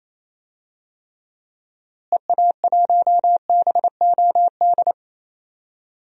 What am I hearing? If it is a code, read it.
Morse code, heard as a keyed tone, about 27 wpm, 700 Hz. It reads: EA1BOD
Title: DXCC-100 Call Signs - 6 Characters - Repeated 2x - 28wpm